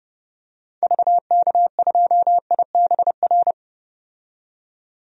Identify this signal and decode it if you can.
Morse code, heard as a keyed tone, about 32 wpm, 700 Hz. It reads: VK2IBR
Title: DXCC-100 Call Signs - 6 Characters - 30wpm